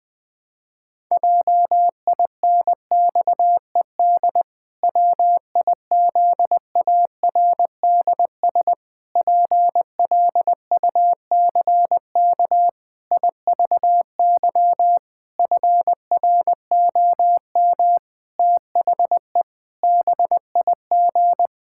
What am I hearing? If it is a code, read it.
Morse code, heard as a keyed tone, about 19 wpm, 700 Hz. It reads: JINXED WIZARDS PLUCK IVY FROM THE BIG
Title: Pangram Challenge v2 - 20wpm